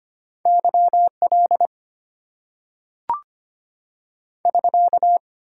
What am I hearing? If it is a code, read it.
Morse code, heard as a keyed tone, about 25 wpm, 700 Hz. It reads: YL <SK>
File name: Top 90 QSO Elements - 8x Character Spacing - 25wpm